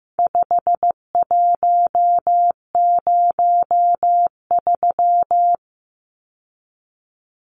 Morse code, heard as a keyed tone, about 15 wpm, 700 Hz. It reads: 5103
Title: Numbers - 4-Digit 15wpm